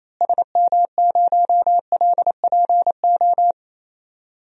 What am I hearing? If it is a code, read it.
Morse code, heard as a keyed tone, about 28 wpm, 700 Hz. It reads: SM0LPO
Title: DXCC-100 Call Signs - 6 Characters - Repeated 2x - 28wpm